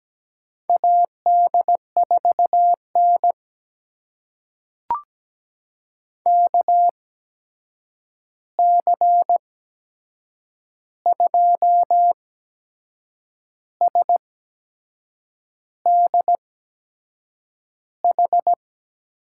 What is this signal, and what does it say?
Morse code, heard as a keyed tone, about 17 wpm, spaced as for 6 wpm, 700 Hz. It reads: AD4N KC2SDH
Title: USA Call Signs - 8x Character Spacing - 17wpm